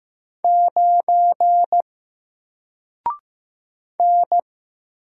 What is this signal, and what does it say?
Morse code, heard as a keyed tone, about 15 wpm, 700 Hz. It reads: 9 N
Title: Single Character (Letter:Number:Punctuation:Prosign) 15wpm